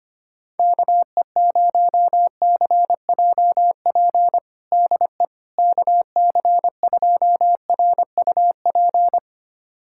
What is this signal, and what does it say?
Morse code, heard as a keyed tone, about 25 wpm, 700 Hz. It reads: KE0CJP DE KC2RUP
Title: USA Callsign DE Callsign - 6 Characters - 25wpm